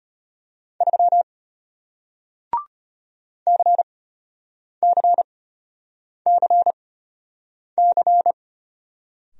Morse code, heard as a keyed tone, about 34 wpm, spaced as for 8 wpm, 700 Hz. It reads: 3 CCCC